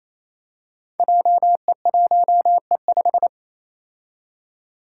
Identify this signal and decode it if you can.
Morse code, heard as a keyed tone, about 28 wpm, 700 Hz. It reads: JE1E5
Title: Five Challenging Letter-Number Combinations 28wpm